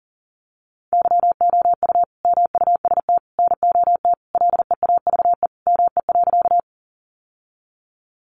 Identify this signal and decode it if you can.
Morse code, heard as a keyed tone, about 40 wpm, 700 Hz. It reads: YOU MUST NOT LEAVE ME.
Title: Sentences from Top 600 Words 40wpm